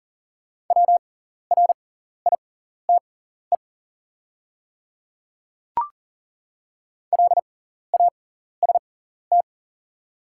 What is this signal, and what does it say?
Morse code, heard as a keyed tone, about 40 wpm, spaced as for 14 wpm, 700 Hz. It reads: WRITE LAST
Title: 6x Character Spacing - Top 300 Words - 40wpm